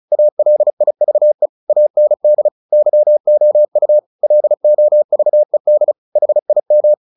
Morse code, heard as a keyed tone, about 34 wpm, 600 Hz. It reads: ALIVE AND YOU LOVED HIM
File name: the old man and the sea 2 35wpm